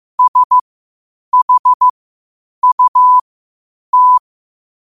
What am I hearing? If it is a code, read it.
Morse code, heard as a keyed tone, about 15 wpm, spaced as for 8 wpm, 1000 Hz. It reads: SHUT
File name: qotdfiles / 8wpm